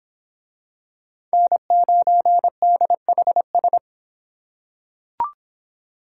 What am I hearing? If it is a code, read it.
Morse code, heard as a keyed tone, about 26 wpm, 700 Hz. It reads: N9DHS